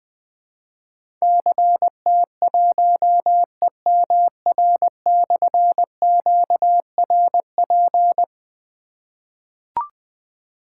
Morse code, heard as a keyed tone, about 20 wpm, 700 Hz. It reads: CT1EMR/QRP
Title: Difficult Call Signs 20wpm